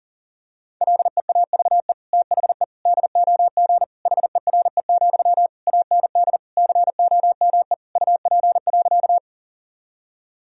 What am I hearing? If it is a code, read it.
Morse code, heard as a keyed tone, about 37 wpm, 700 Hz. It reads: LEAVE THE DOG HERE, AND COME UP.
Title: Sentences from Top 1000 Words - Review Part 2 - 40wpm